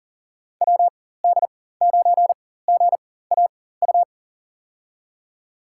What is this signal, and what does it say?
Morse code, heard as a keyed tone, about 40 wpm, spaced as for 19 wpm, 700 Hz. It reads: WD9GAU
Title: USA Call Signs - 6 Characters - 4x Character Spacing - 40wpm